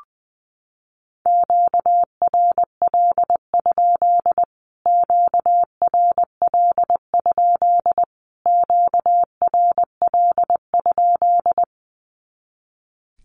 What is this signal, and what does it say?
Morse code, heard as a keyed tone, about 20 wpm, 700 Hz. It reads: QRL? QRL? QRL?
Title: Q-Codes Repeated 3x 20wpm